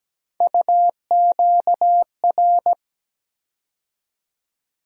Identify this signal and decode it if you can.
Morse code, heard as a keyed tone, about 17 wpm, 700 Hz. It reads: UQR
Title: USA Call Sign Postfixes - 3 Letters 17wpm